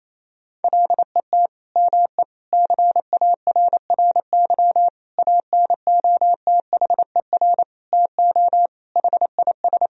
Morse code, heard as a keyed tone, about 28 wpm, 700 Hz. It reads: LET ME CARRY ANOTHER TO HIS
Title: Sentences from Top 1000 Words - Review Part 1 - 28wpm